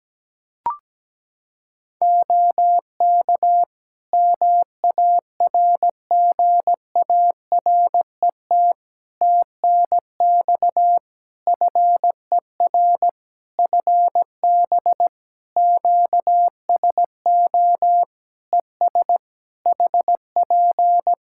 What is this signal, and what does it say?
Morse code, heard as a keyed tone, about 17 wpm, 700 Hz. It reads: OK MARGARET TNX FER FB QSO ES HP
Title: Practice QSO - Exchange 4 - Ending - 17wpm